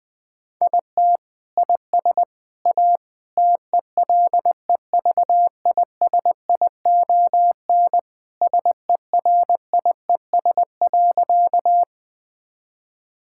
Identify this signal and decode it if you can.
Morse code, heard as a keyed tone, about 20 wpm, 700 Hz. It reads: IT IS A TELEVISION SERIES.
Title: Sentences from Top 1000 Words - Review Part 5 - 20wpm